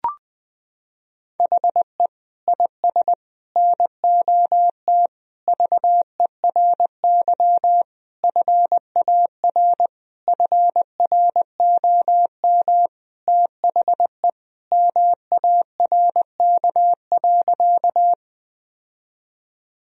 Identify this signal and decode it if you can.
Morse code, heard as a keyed tone, about 20 wpm, 700 Hz. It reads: HE IS NOT VERY FAR FROM THE MARK.
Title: Sentences from Top 1000 Words - Review Part 1 - 20wpm